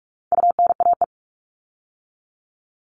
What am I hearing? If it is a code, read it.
Morse code, heard as a keyed tone, about 44 wpm, 700 Hz. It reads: UNAE
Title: All Cut Numbers - 4-Digit 45wpm